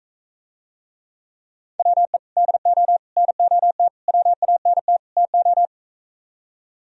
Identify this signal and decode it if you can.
Morse code, heard as a keyed tone, about 45 wpm, 700 Hz. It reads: WE DO NOT WANT TO